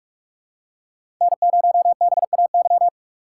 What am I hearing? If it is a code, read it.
Morse code, heard as a keyed tone, about 42 wpm, 700 Hz. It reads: N0BAY
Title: USA Call Signs - 5 Characters - 6x Character Spacing - 45wpm